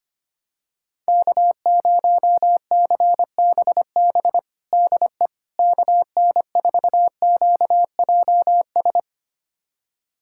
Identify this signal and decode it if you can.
Morse code, heard as a keyed tone, about 25 wpm, 700 Hz. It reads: K0CBB DE KN4QJS
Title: USA Callsign DE Callsign 25wpm